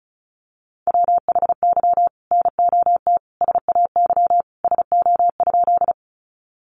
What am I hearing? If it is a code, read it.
Morse code, heard as a keyed tone, about 34 wpm, 700 Hz. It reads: WHY NOT SAY SO?
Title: Sentences from Top 600 Words 35wpm